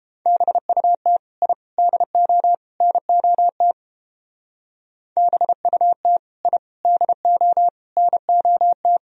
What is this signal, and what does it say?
Morse code, heard as a keyed tone, about 31 wpm, 700 Hz. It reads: BUT I DO NOT BUT I DO NOT